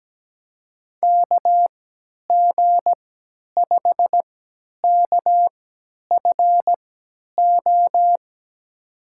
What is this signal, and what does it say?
Morse code, heard as a keyed tone, about 17 wpm, spaced as for 10 wpm, 700 Hz. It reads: KG5KFO
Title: USA Call Signs - 6 Characters - 3x Character Spacing - 17wpm